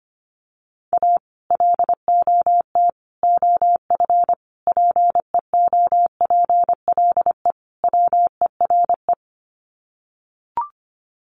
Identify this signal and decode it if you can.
Morse code, heard as a keyed tone, about 25 wpm, 700 Hz. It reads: A LOT OF PEOPLE WERE
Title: Sets of 5 Words Encore2 25wpm